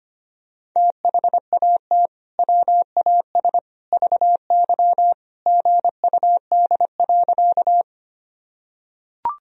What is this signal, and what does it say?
Morse code, heard as a keyed tone, about 25 wpm, 700 Hz. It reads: THAT WAS VY GUD.
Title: Sentences from Top 200 Words and Abbreviations 25wpm